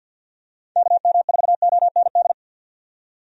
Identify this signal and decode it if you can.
Morse code, heard as a keyed tone, about 49 wpm, 700 Hz. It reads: KM4OND
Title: USA Call Signs - 6 Characters - 50wpm